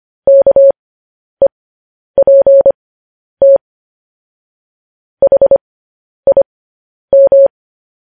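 Morse code, heard as a keyed tone, about 25 wpm, spaced as for 10 wpm, 550 Hz. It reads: KEPT HIM
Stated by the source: Morse code news headlines transmission.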